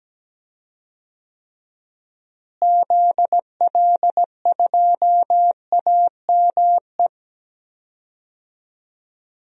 Morse code, heard as a keyed tone, about 17 wpm, 700 Hz. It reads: ZL2AME
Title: DXCC-100 Call Signs - 6 Characters - Repeated 2x - 17wpm